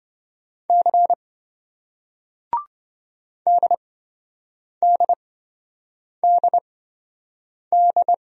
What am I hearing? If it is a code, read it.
Morse code, heard as a keyed tone, about 27 wpm, spaced as for 7 wpm, 700 Hz. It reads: C DDDD